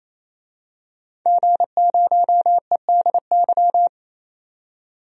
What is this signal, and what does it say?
Morse code, heard as a keyed tone, about 27 wpm, 700 Hz. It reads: G0EDY
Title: DXCC-100 Call Signs - 5 Characters - 28wpm